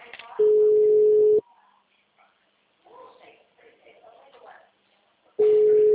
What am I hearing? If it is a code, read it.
Ringback tone (Europe, 425 Hz), with speech.
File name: out- -2009- - - .10